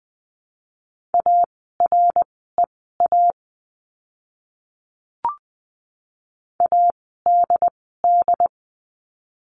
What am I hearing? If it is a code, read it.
Morse code, heard as a keyed tone, about 20 wpm, spaced as for 14 wpm, 700 Hz. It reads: AREA ADD
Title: 101 Lesson 17f - Letter u - Words 20wpm